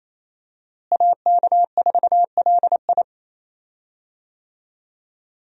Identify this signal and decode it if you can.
Morse code, heard as a keyed tone, about 28 wpm, 700 Hz. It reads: AK4LI
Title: USA Call Signs - 5 Characters - Repeated 3x - 28wpm